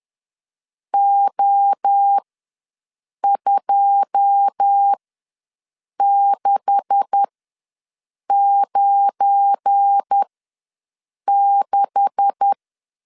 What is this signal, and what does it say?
Morse code, heard as a keyed tone, about 10 wpm, spaced as for 6 wpm, 800 Hz. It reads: O2696
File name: LEZIONI / 9 286 ws 10wpm 4032